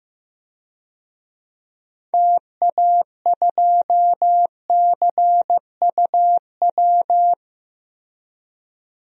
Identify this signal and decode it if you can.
Morse code, heard as a keyed tone, about 15 wpm, 700 Hz. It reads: TA2CUW
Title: DXCC-100 Call Signs - 6 Characters - Repeated 2x - 15wpm